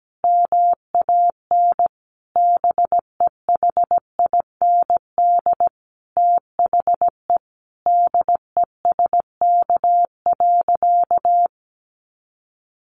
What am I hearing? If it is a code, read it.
Morse code, heard as a keyed tone, about 17 wpm, 700 Hz. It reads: MAN BEHIND THE DESK.
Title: A Sound Of Thunder 17wpm